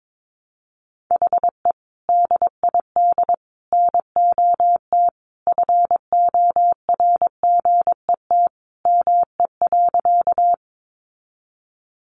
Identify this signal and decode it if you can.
Morse code, heard as a keyed tone, about 22 wpm, 700 Hz. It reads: HE DID NOT FORGET ME.
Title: Sentences from Top 1200 Words 22wpm